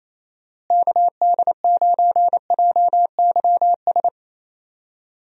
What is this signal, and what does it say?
Morse code, heard as a keyed tone, about 28 wpm, 700 Hz. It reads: KD9JYS
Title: USA Call Signs - 6 Characters - Repeated 3x - 28wpm